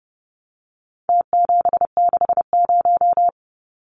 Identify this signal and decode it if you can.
Morse code, heard as a keyed tone, about 30 wpm, 700 Hz. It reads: T760
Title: Numbers - Contest Style - 4-Digit 30wpm